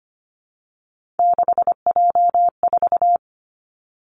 Morse code, heard as a keyed tone, about 25 wpm, 700 Hz. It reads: 6J4
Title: Three Challenging Letter-Number Combinations 25wpm